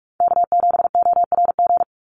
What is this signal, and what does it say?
Morse code, heard as a keyed tone, about 44 wpm, 700 Hz. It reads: K7ORG
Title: NA SOTA Chaser Callsigns 45wpm